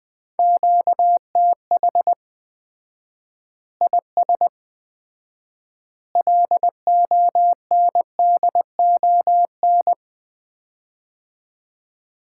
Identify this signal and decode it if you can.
Morse code, heard as a keyed tone, about 20 wpm, 700 Hz. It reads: QTH IS LONDON
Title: Common QSO Phrases - 4x Word-Spacing - 20wpm